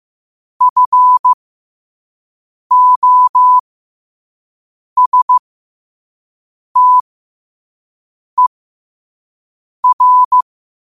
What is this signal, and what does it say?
Morse code, heard as a keyed tone, about 15 wpm, spaced as for 5 wpm, 1000 Hz. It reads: FOSTER
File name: qotdfiles / 5wpm